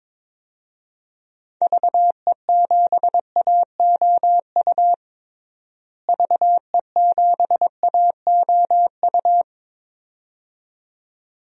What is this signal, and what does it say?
Morse code, heard as a keyed tone, about 21 wpm, 700 Hz. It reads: VE7AOU VE7AOU
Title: DXCC-100 Call Signs - 6 Characters - Repeated 2x - 22wpm